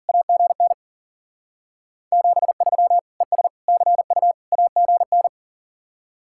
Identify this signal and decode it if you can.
Morse code, heard as a keyed tone, about 49 wpm, 700 Hz. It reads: AGN 73 ES CU AGN
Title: Speed-Racing Common QSO Phrases - 40wpm